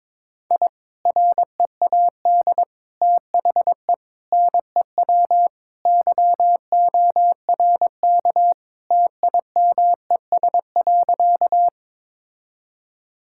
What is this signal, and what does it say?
Morse code, heard as a keyed tone, about 22 wpm, 700 Hz. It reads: I READ THE NEW YORK TIMES.
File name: practice / 22wpm / Sentences with 3-Word Phrases - 22wpm